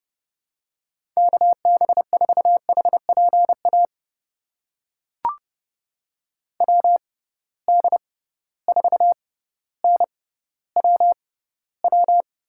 Morse code, heard as a keyed tone, about 32 wpm, spaced as for 16 wpm, 700 Hz. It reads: KB4HPA WD4NWW
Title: USA Call Signs - 6 Characters - 6x Character Spacing - 30wpm